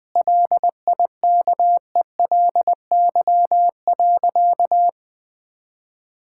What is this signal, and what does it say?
Morse code, heard as a keyed tone, about 20 wpm, 700 Hz. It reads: LIKELY.
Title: Sentences from Top 1600 Words 20wpm